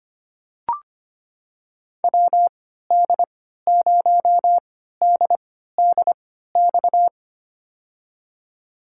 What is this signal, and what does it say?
Morse code, heard as a keyed tone, about 25 wpm, spaced as for 14 wpm, 700 Hz. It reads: WD0DDX
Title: USA Call Signs - 3x Character Spacing - 25wpm